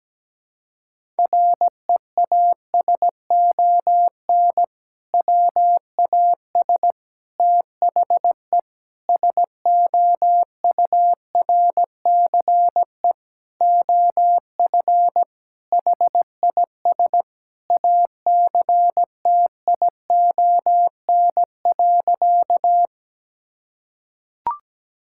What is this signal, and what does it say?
Morse code, heard as a keyed tone, about 17 wpm, 700 Hz. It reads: REASON WAS THE SOURCE OF HIS ACTION.
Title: Sentences from Top 1000 Words - Review Part 5 - 17wpm